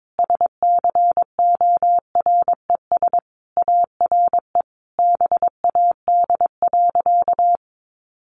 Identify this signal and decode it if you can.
Morse code, heard as a keyed tone, about 22 wpm, 700 Hz. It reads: SCORES ARE BAD.
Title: Additional Easy Phrases 22wpm